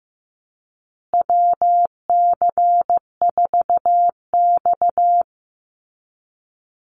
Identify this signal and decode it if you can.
Morse code, heard as a keyed tone, about 15 wpm, 700 Hz. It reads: WC4X
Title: Call Signs Repeated 2x - 15wpm